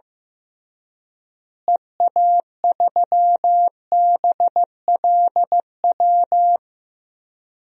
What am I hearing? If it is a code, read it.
Morse code, heard as a keyed tone, about 16 wpm, 700 Hz. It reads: EA3BLW
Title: DXCC-100 Call Signs - 6 Characters - Repeated 3x - 15wpm